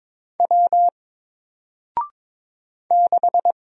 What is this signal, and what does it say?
Morse code, heard as a keyed tone, about 22 wpm, 700 Hz. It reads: W 6
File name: Single-Letter-Number-22wpm